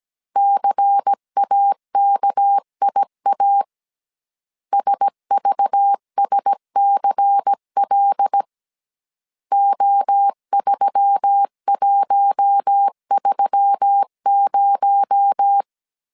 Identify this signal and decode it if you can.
Morse code, heard as a keyed tone, about 17 wpm, 800 Hz. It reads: CAKIA SVSCL O3130
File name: 16wpm